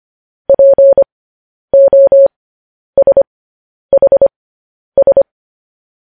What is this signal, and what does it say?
Morse code, heard as a keyed tone, about 25 wpm, spaced as for 10 wpm, 550 Hz. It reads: POSHS